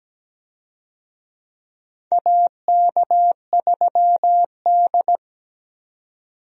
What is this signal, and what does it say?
Morse code, heard as a keyed tone, about 17 wpm, 700 Hz. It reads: AK3D
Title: USA Call Signs - 4 Characters - 17wpm